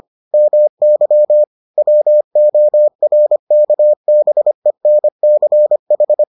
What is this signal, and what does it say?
Morse code, heard as a keyed tone, about 25 wpm, 600 Hz. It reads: MY WORKBENCH